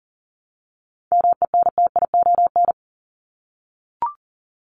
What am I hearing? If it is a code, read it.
Morse code, heard as a keyed tone, about 40 wpm, 700 Hz. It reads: MENTION
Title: Top 1501-1750 Words 40wpm